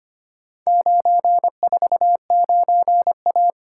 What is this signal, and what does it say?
Morse code, heard as a keyed tone, about 25 wpm, 700 Hz. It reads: 949A
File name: practice / 25wpm / Four Letter-Number Combinations - 25wpm